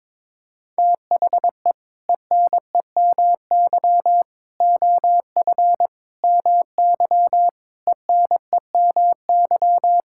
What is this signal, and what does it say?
Morse code, heard as a keyed tone, about 22 wpm, 700 Hz. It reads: THE ENEMY OF MY ENEMY
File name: practice / 22wpm / Sentences with 8-Word Phrases - 22wpm